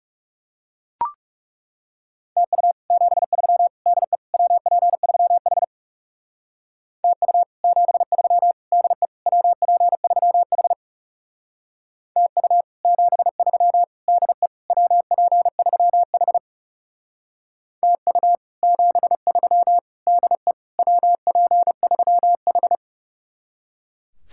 Morse code, heard as a keyed tone, about 37 wpm, 700 Hz. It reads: TU 73 DE WP3H TU 73 DE WP3H TU 73 DE WP3H TU 73 DE WP3H
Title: Speed-Racing Common QSO Phrases - 30wpm